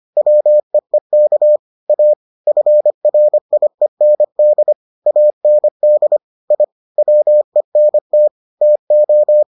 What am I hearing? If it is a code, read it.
Morse code, heard as a keyed tone, about 25 wpm, 600 Hz. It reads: WEEK A FRIEND AND I WENT TO